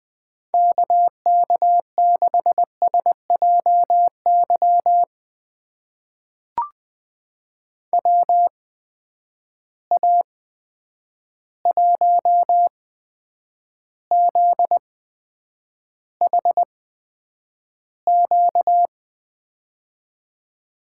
Morse code, heard as a keyed tone, about 20 wpm, spaced as for 9 wpm, 700 Hz. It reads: KK6SJY WA1ZHQ
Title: USA Call Signs - 6 Characters - 8x Character Spacing - 20wpm